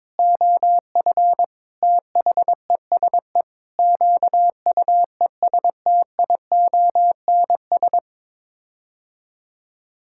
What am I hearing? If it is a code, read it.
Morse code, heard as a keyed tone, about 22 wpm, 700 Hz. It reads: OF THESE QUESTIONS
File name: Sets of 9 Words - 22wpm